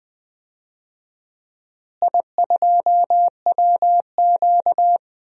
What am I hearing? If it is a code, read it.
Morse code, heard as a keyed tone, about 20 wpm, 700 Hz. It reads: I2WQ
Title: DXCC-100 Call Signs - 4 Characters - Repeated 2x - 20wpm